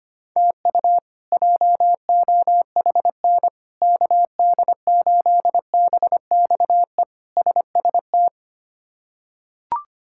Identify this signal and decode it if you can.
Morse code, heard as a keyed tone, about 25 wpm, 700 Hz. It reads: TU JOHN KD8BXE SST
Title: K1USN SST QSO Practice - 25wpm